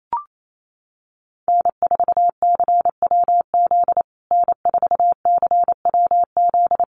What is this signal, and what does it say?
Morse code, heard as a keyed tone, about 28 wpm, 700 Hz. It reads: N4CWZ N4CWZ
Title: USA Call Signs Repeated 3x 28wpm